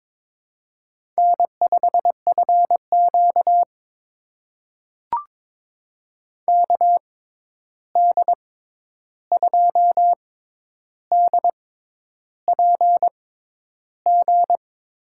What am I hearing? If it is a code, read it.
Morse code, heard as a keyed tone, about 22 wpm, spaced as for 10 wpm, 700 Hz. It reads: N5FQ KD2DPG
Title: USA Call Signs - 6x Character Spacing - 22wpm